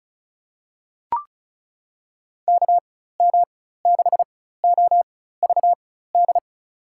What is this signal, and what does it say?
Morse code, heard as a keyed tone, about 34 wpm, spaced as for 16 wpm, 700 Hz. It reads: KM6OVD
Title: USA Call Signs - 6 Characters - 4x Character Spacing - 35wpm